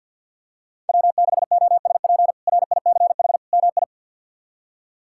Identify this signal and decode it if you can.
Morse code, heard as a keyed tone, about 52 wpm, 700 Hz. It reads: W6OSP RICH MI
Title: K1USN SST QSO Practice - 50wpm